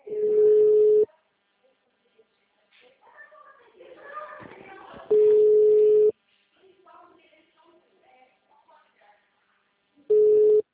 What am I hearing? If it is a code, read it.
Ringback tone (Europe, 425 Hz), with speech.
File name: out- -2020- - - .36368